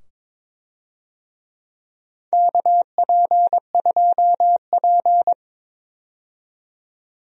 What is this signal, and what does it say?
Morse code, heard as a keyed tone, about 22 wpm, 700 Hz. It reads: KP2P
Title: DXCC-100 Call Signs - 4 Characters - 22wpm